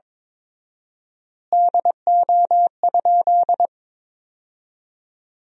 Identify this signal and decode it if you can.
Morse code, heard as a keyed tone, about 22 wpm, 700 Hz. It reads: DO?
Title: 4x Word-Spacing - Sentences from Top 100 Words - 22wpm